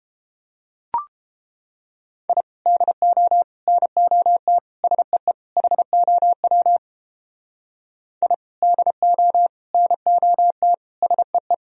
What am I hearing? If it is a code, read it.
Morse code, heard as a keyed tone, about 29 wpm, 700 Hz. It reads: I DO NOT SEE HOW I DO NOT SEE